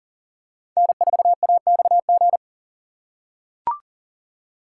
Morse code, heard as a keyed tone, about 40 wpm, 700 Hz. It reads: N4AXG
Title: USA Call Signs - 5 Characters - 6x Character Spacing - 40wpm